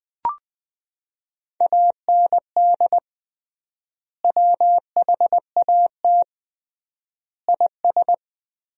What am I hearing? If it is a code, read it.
Morse code, heard as a keyed tone, about 21 wpm, 700 Hz. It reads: AND WHAT IS
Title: 3x Word-Spacing - Sentences from Top 100 Words - 20wpm